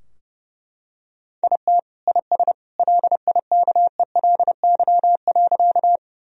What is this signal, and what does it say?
Morse code, heard as a keyed tone, about 30 wpm, 700 Hz. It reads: IT IS LIKELY.
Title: Sentences from Top 1000 Words - Review Part 4 - 30wpm